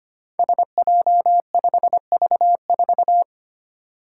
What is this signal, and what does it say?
Morse code, heard as a keyed tone, about 25 wpm, 700 Hz. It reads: SJ5V4
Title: Five Challenging Letter-Number Combinations 25wpm